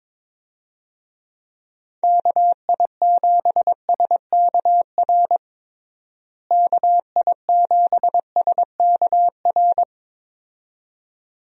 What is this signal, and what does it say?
Morse code, heard as a keyed tone, about 22 wpm, 700 Hz. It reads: KI7SKR KI7SKR
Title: USA Call Signs - 6 Characters - Repeated 2x - 22wpm